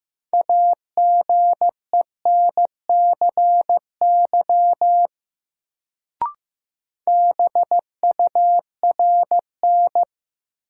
Morse code, heard as a keyed tone, about 16 wpm, 700 Hz. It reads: AGENCY BURN
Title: Sentences from Top 1300 Words - New Word Drill Alt - 15wpm